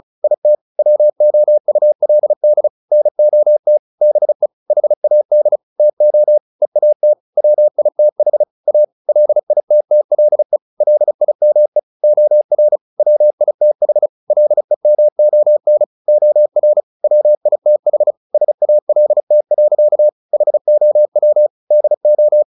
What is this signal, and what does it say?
Morse code, heard as a keyed tone, about 34 wpm, 600 Hz. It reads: IT WOULD NOT BE HAD TO EAT WITH A LITTLE LIME OR WITH LEMON OR WITH SALT. HOW DO
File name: the old man and the sea / the old man and the sea 1 35wpm